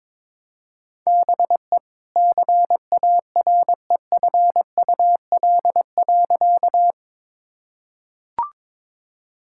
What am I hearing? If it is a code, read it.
Morse code, heard as a keyed tone, about 21 wpm, 700 Hz. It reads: BE CAREFUL.
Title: Sentences from Top 1600 Words 22wpm